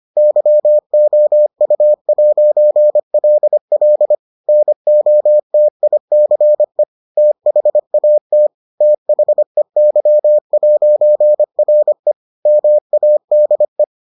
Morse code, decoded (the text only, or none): YOU'LL NOTICE THAT THEY'RE MADE